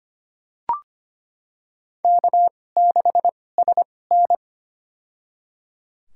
Morse code, heard as a keyed tone, about 25 wpm, spaced as for 18 wpm, 700 Hz. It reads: K6SN
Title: USA Call Signs - 4 Characters - 2x Character Spacing - 25wpm